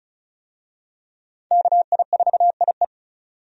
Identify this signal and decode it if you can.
Morse code, heard as a keyed tone, about 36 wpm, 700 Hz. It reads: KI4IE
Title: USA Call Signs - 5 Characters - 35wpm